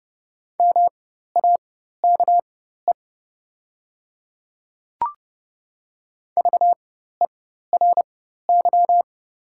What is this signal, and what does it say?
Morse code, heard as a keyed tone, about 29 wpm, spaced as for 14 wpm, 700 Hz. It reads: MAKE VERY
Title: 4x Character Spacing - Top 300 Words - 30wpm